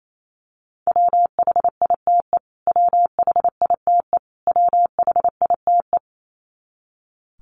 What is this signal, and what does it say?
Morse code, heard as a keyed tone, about 28 wpm, 700 Hz. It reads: WHITE WHITE WHITE
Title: Top 300 Words Repeated 28wpm